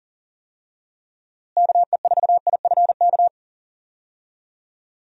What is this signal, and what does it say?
Morse code, heard as a keyed tone, about 37 wpm, 700 Hz. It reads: KE4IFK
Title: USA Call Signs - 6 Characters - Repeated 3x - 40wpm